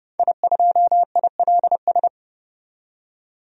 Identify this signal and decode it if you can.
Morse code, heard as a keyed tone, about 30 wpm, 700 Hz. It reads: I2ILS
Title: DXCC-100 Call Signs - 5 Characters - 30wpm